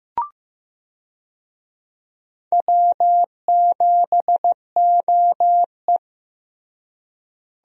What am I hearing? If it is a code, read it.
Morse code, heard as a keyed tone, about 16 wpm, 700 Hz. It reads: W7OE
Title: USA Call Signs - 4 Characters - Repeated 3x - 15wpm